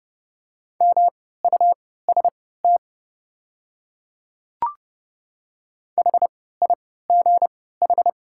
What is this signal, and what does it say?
Morse code, heard as a keyed tone, about 30 wpm, spaced as for 17 wpm, 690 Hz. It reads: MUST HIGH
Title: 3x Character Spacing - Top 200 Words - 30wpm